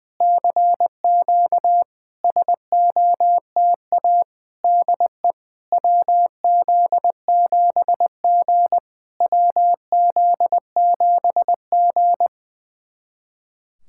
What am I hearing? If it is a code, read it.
Morse code, heard as a keyed tone, about 21 wpm, 700 Hz. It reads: CQ SOTA DE WZ7G WZ7G
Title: SOTA QSO Practice 20wpm